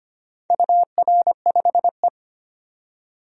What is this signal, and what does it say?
Morse code, heard as a keyed tone, about 25 wpm, 700 Hz. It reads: UR5E
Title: DXCC-100 Call Signs - 4 Characters - 25wpm